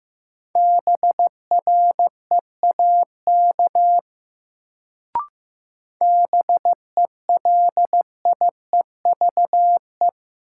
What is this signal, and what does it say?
Morse code, heard as a keyed tone, about 15 wpm, 700 Hz. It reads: BREAK BELIEVE
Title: Sets of 4 Words - New Word Drill - 15wpm